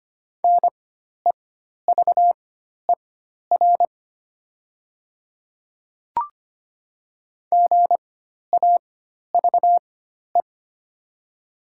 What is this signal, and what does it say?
Morse code, heard as a keyed tone, about 25 wpm, spaced as for 12 wpm, 700 Hz. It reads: NEVER GAVE
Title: 4x Character Spacing - Top 500 Words - 25wpm